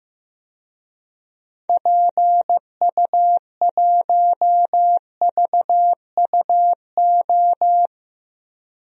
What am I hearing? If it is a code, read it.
Morse code, heard as a keyed tone, about 15 wpm, 700 Hz. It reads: PU1VUO
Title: DXCC-100 Call Signs - 6 Characters - Repeated 3x - 15wpm